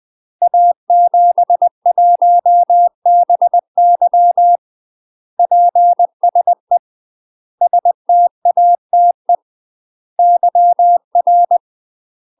Morse code, heard as a keyed tone, about 20 wpm, 700 Hz. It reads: A71BY PSE STATE YR
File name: UK MASTER QSOs TRAINING / 20WPM / UK MASTER QSOs 0034